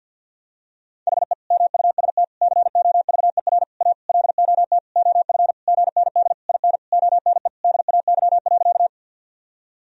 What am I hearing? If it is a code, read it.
Morse code, heard as a keyed tone, about 49 wpm, 700 Hz. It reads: HE MUST COVER A LOT OF GND IN ONE DAY.
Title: Sentences from Top 500 Words and Abbreviations 50wpm